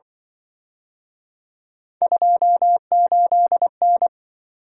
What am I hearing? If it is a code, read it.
Morse code, heard as a keyed tone, about 24 wpm, 700 Hz. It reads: 28N
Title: Speed-Racing - Three Letter-Number Combinations - 20wpm